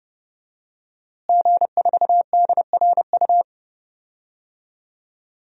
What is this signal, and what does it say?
Morse code, heard as a keyed tone, about 32 wpm, 700 Hz. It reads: G4DRU
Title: DXCC-100 Call Signs - 5 Characters - Repeated 3x - 30wpm